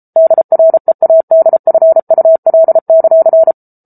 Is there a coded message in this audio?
Yes: Morse code from 0.2 to 3.5 s, heard as a keyed tone, about 34 wpm, 650 Hz. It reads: DREADFUL;